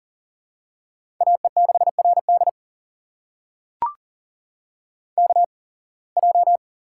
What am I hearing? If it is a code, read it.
Morse code, heard as a keyed tone, about 40 wpm, spaced as for 26 wpm, 700 Hz. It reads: AE6RD KJ
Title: USA Call Signs - 8x Character Spacing - 40wpm